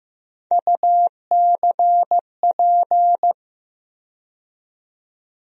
Morse code, heard as a keyed tone, about 15 wpm, 700 Hz. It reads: UCP
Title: USA Call Sign Postfixes - 3 Letters 15wpm